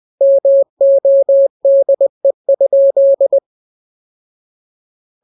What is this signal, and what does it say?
Morse code, heard as a keyed tone, about 20 wpm, 550 Hz. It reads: MODE?
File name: jokes / joke 27 20 0000